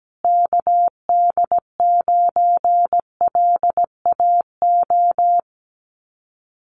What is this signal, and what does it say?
Morse code, heard as a keyed tone, about 17 wpm, 700 Hz. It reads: KD9LAO
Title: Common QSO Phrases Encore 17wpm